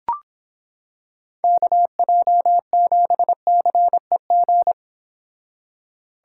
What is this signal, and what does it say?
Morse code, heard as a keyed tone, about 26 wpm, 700 Hz. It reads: KJ7CEG